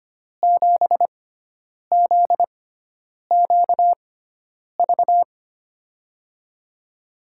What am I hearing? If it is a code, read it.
Morse code, heard as a keyed tone, about 25 wpm, spaced as for 9 wpm, 700 Hz. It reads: 7ZQV
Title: USA Call Signs - 6x Character Spacing - 25wpm